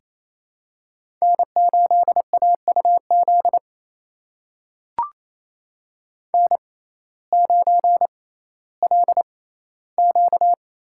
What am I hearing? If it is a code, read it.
Morse code, heard as a keyed tone, about 28 wpm, spaced as for 15 wpm, 700 Hz. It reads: N8AUZ N9LQ
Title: USA Call Signs - 5 Characters - 6x Character Spacing - 28wpm